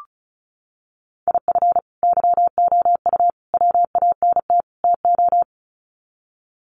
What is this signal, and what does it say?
Morse code, heard as a keyed tone, about 34 wpm, 700 Hz. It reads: IF YOU WANT TO
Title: Sets of 4 Words 35wpm